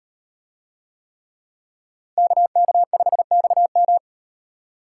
Morse code, heard as a keyed tone, about 38 wpm, 700 Hz. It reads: KK5XM